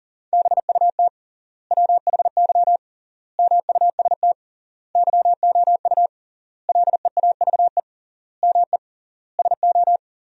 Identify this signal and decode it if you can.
Morse code, heard as a keyed tone, about 40 wpm, 700 Hz. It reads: BUT WHY MUST YOU LEAVE ME SO
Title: 3x Word-Spacing - Sentences from Top 400 Words - 40wpm